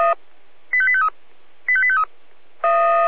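Five-tone selective call (ZVEI), heard as quick runs of five tones, tones of about 70 ms. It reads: 85853 85853
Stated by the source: ZVEI five-tone selective call 85243 received on an SDR.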